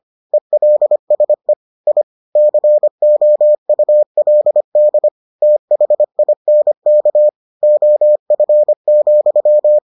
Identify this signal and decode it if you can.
Morse code, heard as a keyed tone, about 25 wpm, 600 Hz. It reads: ELSE I COULD THINK OF,